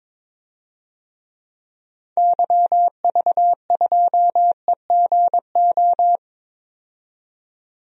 Morse code, heard as a keyed tone, about 22 wpm, 700 Hz. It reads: YV2EGO
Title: DXCC-100 Call Signs - 6 Characters - 22wpm